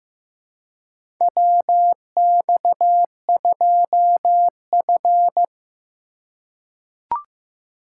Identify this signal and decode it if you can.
Morse code, heard as a keyed tone, about 15 wpm, 700 Hz. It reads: WX2F
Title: USA Call Signs - 4 Characters - 3x Character Spacing - 15wpm